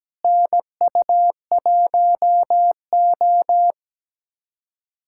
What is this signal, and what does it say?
Morse code, heard as a keyed tone, about 17 wpm, 700 Hz. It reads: NU1O
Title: USA Callsign DE Callsign - 4 Characters - 17wpm